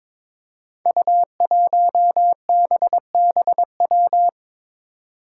Morse code, heard as a keyed tone, about 23 wpm, 700 Hz. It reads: U1BBW
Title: Five Challenging Letter-Number Combinations 22wpm